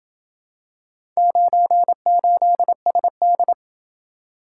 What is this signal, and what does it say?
Morse code, heard as a keyed tone, about 27 wpm, 700 Hz. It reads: 98SD